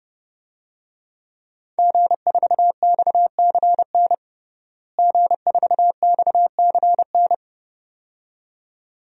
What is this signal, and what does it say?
Morse code, heard as a keyed tone, about 30 wpm, 700 Hz. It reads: G4XCN G4XCN
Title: DXCC-100 Call Signs - 5 Characters - Repeated 2x - 30wpm